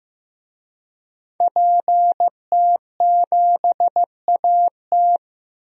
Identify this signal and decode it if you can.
Morse code, heard as a keyed tone, about 15 wpm, 700 Hz. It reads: PT7AT
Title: DXCC-100 Call Signs - 5 Characters - Repeated 2x - 15wpm